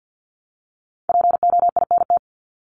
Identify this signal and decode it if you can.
Morse code, heard as a keyed tone, about 49 wpm, 700 Hz. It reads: POINT
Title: Top 200 Words Repeated 50wpm